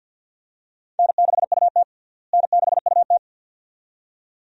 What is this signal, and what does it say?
Morse code, heard as a keyed tone, about 52 wpm, 700 Hz. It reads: N6UT N6UT
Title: USA Call Signs - 4 Characters - Repeated 2x - 50wpm